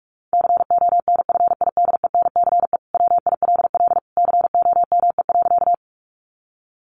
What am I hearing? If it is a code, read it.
Morse code, heard as a keyed tone, about 44 wpm, 700 Hz. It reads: CONFIDENCE WILL COME.
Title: Sentences from Top 1500 Words 45wpm